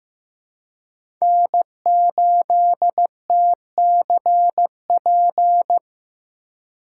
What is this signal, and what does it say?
Morse code, heard as a keyed tone, about 16 wpm, 700 Hz. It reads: N8TCP
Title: USA Call Signs - 5 Characters - 3x Character Spacing - 15wpm